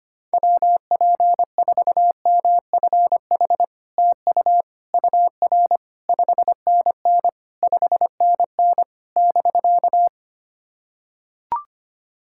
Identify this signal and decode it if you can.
Morse code, heard as a keyed tone, about 25 wpm, 700 Hz. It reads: WP4MFH TU UR 5NN 5NN <BK>
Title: SOTA QSO Practice 25wpm